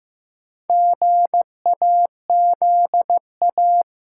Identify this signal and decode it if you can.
Morse code, heard as a keyed tone, about 15 wpm, 700 Hz. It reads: GAZA
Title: Sets of 5 Words - New Word Drill - 15wpm